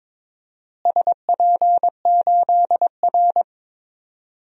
Morse code, heard as a keyed tone, about 22 wpm, 700 Hz. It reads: SP8R
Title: DXCC-100 Call Signs - 4 Characters - Repeated 3x - 22wpm